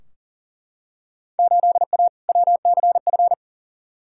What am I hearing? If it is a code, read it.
Morse code, heard as a keyed tone, about 40 wpm, 700 Hz. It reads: 8A WCF
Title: Field Day Exchanges Encore - 40wpm